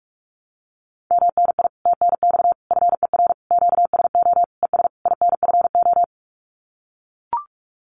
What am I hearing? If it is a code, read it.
Morse code, heard as a keyed tone, about 44 wpm, 700 Hz. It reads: MNI TNX FER QSO ES INFO
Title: Common QSO Phrases Encore 45wpm